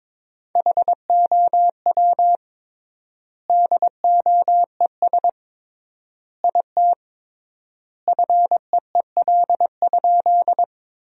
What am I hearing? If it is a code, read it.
Morse code, heard as a keyed tone, about 22 wpm, 700 Hz. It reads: HOW DOES IT FEEL?
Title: 3x Word-Spacing - Sentences from Top 400 Words - 22wpm